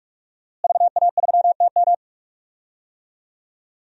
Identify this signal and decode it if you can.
Morse code, heard as a keyed tone, about 46 wpm, 700 Hz. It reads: VA3TM
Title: DXCC-100 Call Signs - 5 Characters - Repeated 2x - 45wpm